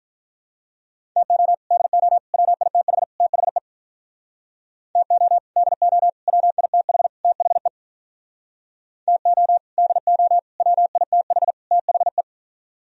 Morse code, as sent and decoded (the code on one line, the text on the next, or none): - --- / -.. --- / .-- .. - .... / - .... / - --- / -.. --- / .-- .. - .... / - .... / - --- / -.. --- / .-- .. - .... / - .... .
TO DO WITH THE TO DO WITH THE TO DO WITH THE